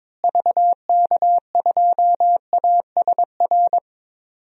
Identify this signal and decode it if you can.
Morse code, heard as a keyed tone, about 22 wpm, 700 Hz. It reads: VK2ASR
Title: DXCC-100 Call Signs - 6 Characters - Repeated 2x - 22wpm